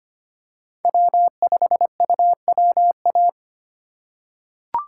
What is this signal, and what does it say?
Morse code, heard as a keyed tone, about 25 wpm, 700 Hz. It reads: W5UWA
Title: USA Call Signs - 2x Character Spacing - 25wpm